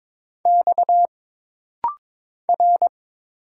Morse code, heard as a keyed tone, about 23 wpm, 700 Hz. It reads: X R
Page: This rapid-fire timing is geared to push your ICR (Instant Character Recognition) skills.